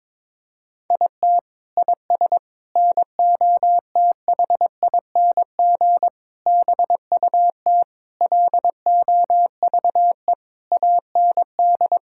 Morse code, heard as a keyed tone, about 22 wpm, 700 Hz. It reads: IT IS NOTHING BUT LOVE AND
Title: Sentences from Top 1600 Words 22wpm